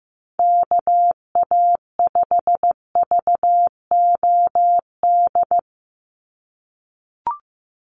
Morse code, heard as a keyed tone, about 15 wpm, 700 Hz. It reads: KA5VOD
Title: USA Call Signs 15wpm